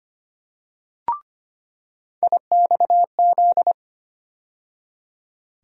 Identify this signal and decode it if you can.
Morse code, heard as a keyed tone, about 25 wpm, 700 Hz. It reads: IXZ
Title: USA Call Sign Postfixes - 3 Letters 25wpm